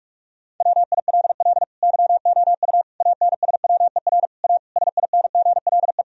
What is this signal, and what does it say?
Morse code, heard as a keyed tone, about 42 wpm, 700 Hz. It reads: WILL YOU ANSWER A SINGLE
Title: Sentences from Top 1000 Words - Review Part 4 - 45wpm